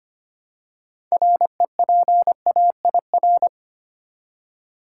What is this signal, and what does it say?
Morse code, heard as a keyed tone, about 25 wpm, 700 Hz. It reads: REPAIR
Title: Six Letter Words - 25wpm